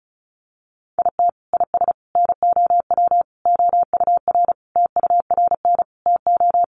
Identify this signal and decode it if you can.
Morse code, heard as a keyed tone, about 34 wpm, 700 Hz. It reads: IT IS NOW OUR TURN TO
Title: Sentences from Top 1400 Words 35wpm